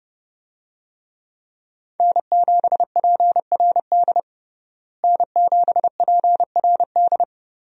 Morse code, heard as a keyed tone, about 32 wpm, 700 Hz. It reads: N7PRD N7PRD
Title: USA Call Signs - 5 Characters - Repeated 2x - 30wpm